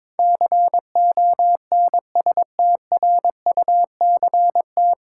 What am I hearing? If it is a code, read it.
Morse code, heard as a keyed tone, about 23 wpm, 700 Hz. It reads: CONSTRUCT
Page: Morse Code Ninja practice 22wpm